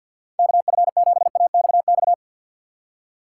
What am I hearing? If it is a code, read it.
Morse code, heard as a keyed tone, about 50 wpm, 700 Hz. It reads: KF7AXX
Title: USA Call Signs - 6 Characters - 6x Character Spacing - 50wpm